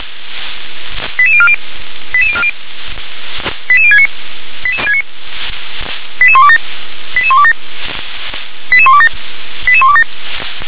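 Five-tone selective call (ZVEI), heard as quick runs of five tones, tones of about 70 ms. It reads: 88040 88040 88069 88127 88127 88127 88127
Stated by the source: ZVEI five-tone call received by SDR.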